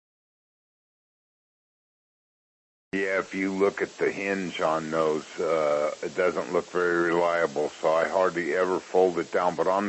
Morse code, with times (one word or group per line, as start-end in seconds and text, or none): none